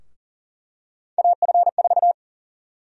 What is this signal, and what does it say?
Morse code, heard as a keyed tone, about 40 wpm, 700 Hz. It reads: AF4
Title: USA Call Sign Prefixes - 40wpm